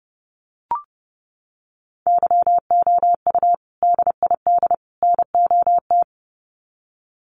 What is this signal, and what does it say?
Morse code, heard as a keyed tone, about 30 wpm, 700 Hz. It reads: YOU DID NOT
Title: Sets of 3 Words 30wpm